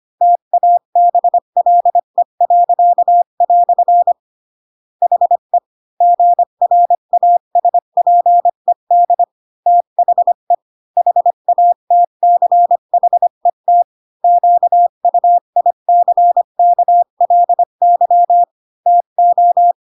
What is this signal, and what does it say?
Morse code, heard as a keyed tone, about 25 wpm, 700 Hz. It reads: TABLE." HE GRASPED THE HATCHET QUICKLY TO
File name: Pinocchio / Pinocchio ch1 20 25 ik1hge